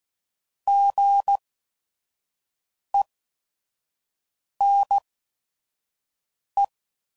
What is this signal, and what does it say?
Morse code, heard as a keyed tone, about 16 wpm, spaced as for 5 wpm, 800 Hz. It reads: GENE
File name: Prefixes / By First Letter / Gs